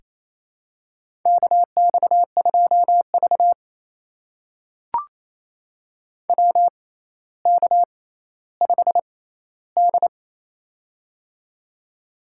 Morse code, heard as a keyed tone, about 29 wpm, spaced as for 15 wpm, 700 Hz. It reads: KX2V WK5D
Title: USA Call Signs - 4 Characters - 6x Character Spacing - 28wpm